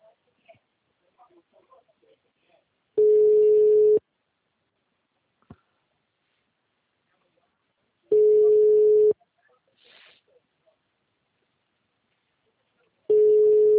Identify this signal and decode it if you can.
Ringback tone (Europe, 425 Hz), with speech.